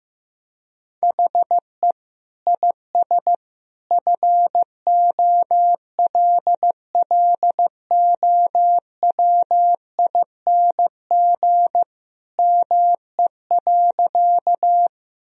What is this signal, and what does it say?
Morse code, heard as a keyed tone, about 15 wpm, 700 Hz. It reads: HE IS FOLLOWING ME.
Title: Sentences from Top 1000 Words - Review Part 4 - 15wpm